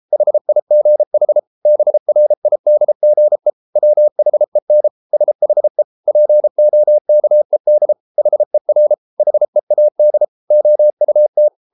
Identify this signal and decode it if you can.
Morse code, heard as a keyed tone, about 33 wpm, 600 Hz. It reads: HIGH BRIDGE WHEN SHE POKED HER HEAD OUT